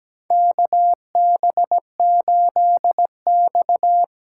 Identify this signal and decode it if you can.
Morse code, heard as a keyed tone, about 17 wpm, 700 Hz. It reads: KB8X
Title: USA Call Signs - 4x Character Spacing - 17wpm